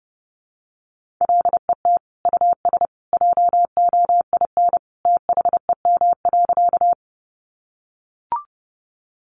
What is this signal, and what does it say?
Morse code, heard as a keyed tone, about 30 wpm, 700 Hz. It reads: LET US JOIN THEM.
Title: Sentences from Top 1400 Words 30wpm